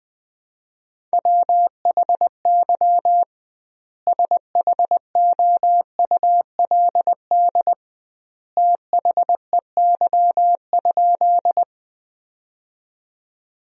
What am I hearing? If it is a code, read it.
Morse code, heard as a keyed tone, about 20 wpm, 700 Hz. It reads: WHY SHOULD THEY?
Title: 2x Word-Spacing - Sentences from Top 300 Words - 20wpm